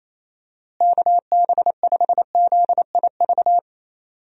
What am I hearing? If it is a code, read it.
Morse code, heard as a keyed tone, about 28 wpm, 700 Hz. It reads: KB5ZIV
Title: USA Call Signs - 6 Characters - Repeated 3x - 28wpm